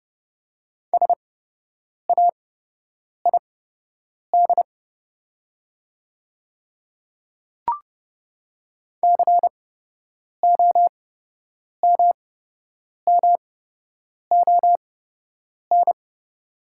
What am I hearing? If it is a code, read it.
Morse code, heard as a keyed tone, about 30 wpm, spaced as for 8 wpm, 700 Hz. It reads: SAID COMMON
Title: 8x Character Spacing - Top 500 Words - 30wpm